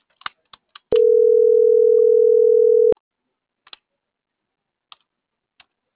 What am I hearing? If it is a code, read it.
Ringback tone (North America), 440 + 480 Hz.